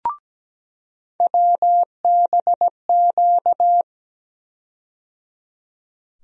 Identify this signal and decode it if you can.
Morse code, heard as a keyed tone, about 17 wpm, 700 Hz. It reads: WBQ
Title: USA Call Sign Postfixes - 3 Letters 17wpm